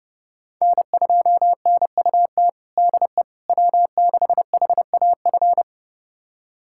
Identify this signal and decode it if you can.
Morse code, heard as a keyed tone, about 32 wpm, 700 Hz. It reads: N2NUT DE W6HAF
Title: USA Callsign DE Callsign - 5 Characters - 30wpm